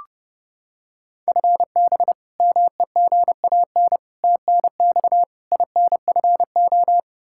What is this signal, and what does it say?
Morse code, heard as a keyed tone, about 30 wpm, 700 Hz. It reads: FB MEGAN TNX INFO
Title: Practice QSO - Exchange 3 - More Info - 30wpm